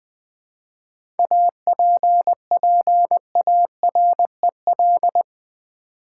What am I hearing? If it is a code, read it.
Morse code, heard as a keyed tone, about 20 wpm, 700 Hz. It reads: APPAREL
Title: 7 Letter Words - First 3 Letters Given - 20wpm